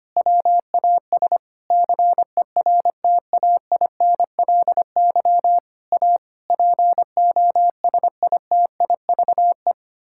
Morse code, heard as a keyed tone, about 25 wpm, 700 Hz. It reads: WAS CERTAINLY A POSITIVE
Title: Sentences from Top 1000 Words - Review Part 6 - 25wpm